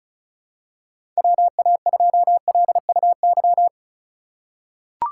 Morse code, heard as a keyed tone, about 34 wpm, 700 Hz. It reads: WA2LUY
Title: USA Call Signs - 6 Characters - 4x Character Spacing - 35wpm